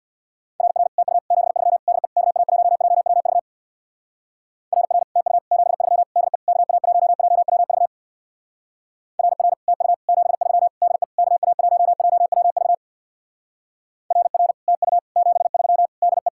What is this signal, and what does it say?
Morse code, heard as a keyed tone, about 64 wpm, 700 Hz. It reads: RR TU 73 DE KA1JWV RR TU 73 DE KA1JWV RR TU 73 DE KA1JWV RR TU 73 DE